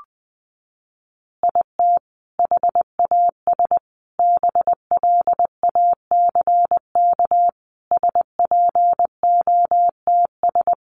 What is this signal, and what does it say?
Morse code, heard as a keyed tone, about 20 wpm, 700 Hz. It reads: IT HAS BLACK SPOTS
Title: Additional Easy Phrases 20wpm